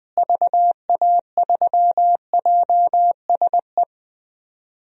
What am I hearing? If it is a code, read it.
Morse code, heard as a keyed tone, about 21 wpm, 700 Hz. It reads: VA3JSE
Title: DXCC-100 Call Signs - 6 Characters - Repeated 3x - 20wpm